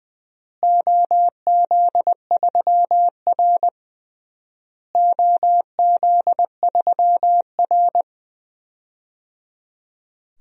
Morse code, heard as a keyed tone, about 20 wpm, 700 Hz. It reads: OZ3R OZ3R
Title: DXCC-100 Call Signs - 4 Characters - Repeated 3x - 20wpm